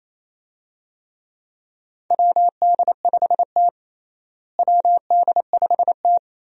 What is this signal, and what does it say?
Morse code, heard as a keyed tone, about 28 wpm, 700 Hz. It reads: WD5T WD5T
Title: USA Call Signs - 4 Characters - Repeated 2x - 28wpm